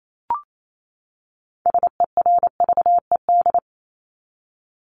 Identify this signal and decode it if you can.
Morse code, heard as a keyed tone, about 28 wpm, 700 Hz. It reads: SERVED
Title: ED Suffix - Top 100 Words with Base Words 28wpm